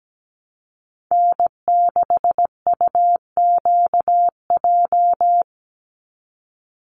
Morse code, heard as a keyed tone, about 17 wpm, 700 Hz. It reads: N6UQJ
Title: USA Call Signs 17wpm